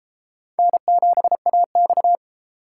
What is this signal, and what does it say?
Morse code, heard as a keyed tone, about 33 wpm, 700 Hz. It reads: N7AX